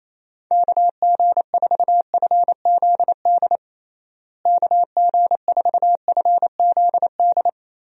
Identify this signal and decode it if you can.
Morse code, heard as a keyed tone, about 27 wpm, 700 Hz. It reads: KG4FZD KG4FZD
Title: USA Call Signs - 6 Characters - Repeated 3x - 28wpm